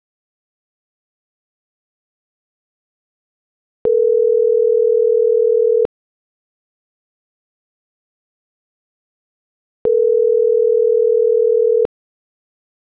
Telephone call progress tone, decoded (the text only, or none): ringback tone